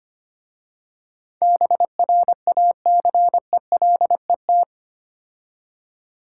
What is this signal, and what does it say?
Morse code, heard as a keyed tone, about 29 wpm, 700 Hz. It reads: BRACELET
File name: Easy Words - Speed-Racing with Hints - 25wpm